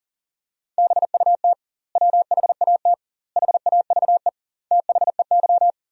Morse code, heard as a keyed tone, about 37 wpm, 700 Hz. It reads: BUT WHAT HAVE THEY
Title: 2x Word-Spacing - Sentences from Top 100 Words - 40wpm